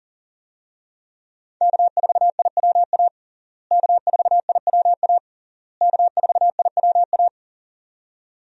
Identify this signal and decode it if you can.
Morse code, heard as a keyed tone, about 40 wpm, 700 Hz. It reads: K4IWA K4IWA K4IWA
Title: USA Call Signs - 5 Characters - Repeated 3x - 40wpm